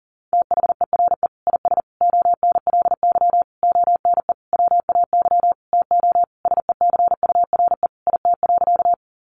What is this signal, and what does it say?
Morse code, heard as a keyed tone, about 40 wpm, 700 Hz. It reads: THERE IS ONLY ONE WAY TO SECURE IT.
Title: Sentences from Top 1200 Words 40wpm